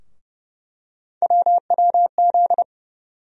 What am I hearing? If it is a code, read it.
Morse code, heard as a keyed tone, about 30 wpm, 700 Hz. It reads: WWZ
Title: USA Call Sign Postfixes - 3 Letters 30wpm